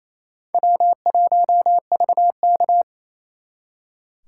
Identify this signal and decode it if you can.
Morse code, heard as a keyed tone, about 35 wpm, 700 Hz. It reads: W1VK
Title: Speed Racing - USA 1x2 Call Signs 28wpm